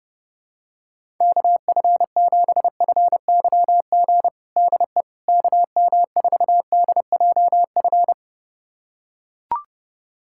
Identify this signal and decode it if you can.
Morse code, heard as a keyed tone, about 30 wpm, 700 Hz. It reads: KF7FYG DE KM4DJF
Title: USA Callsign DE Callsign 30wpm